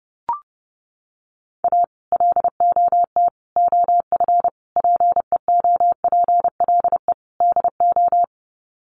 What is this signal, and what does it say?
Morse code, heard as a keyed tone, about 30 wpm, 700 Hz. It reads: A LOT OF PEOPLE DO
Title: Sets of 5 Words 30wpm